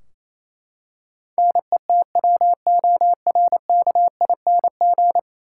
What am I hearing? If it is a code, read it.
Morse code, heard as a keyed tone, about 28 wpm, 700 Hz. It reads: NETWORKING
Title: Practice QSO - Exchange 3 - More Info - 28wpm